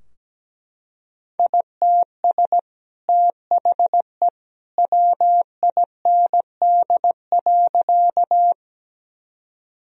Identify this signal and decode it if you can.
Morse code, heard as a keyed tone, about 17 wpm, 700 Hz. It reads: ITS THE WIND.
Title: Sentences from Top 1000 Words - Review Part 2 - 17wpm